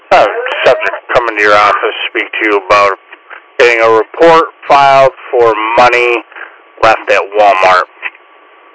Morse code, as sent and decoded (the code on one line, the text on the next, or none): none